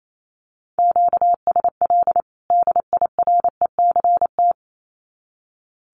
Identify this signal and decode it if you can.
Morse code, heard as a keyed tone, about 28 wpm, 700 Hz. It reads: QSL DIRECT
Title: Common QSO Phrases 28wpm